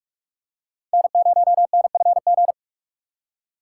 Morse code, heard as a keyed tone, about 46 wpm, 700 Hz. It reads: N0NFG
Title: USA Call Signs - 5 Characters - 2x Character Spacing - 45wpm